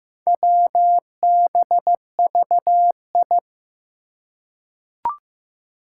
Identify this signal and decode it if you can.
Morse code, heard as a keyed tone, about 15 wpm, 700 Hz. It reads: WBVI
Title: Four Challenging Letter-Number Combinations 15wpm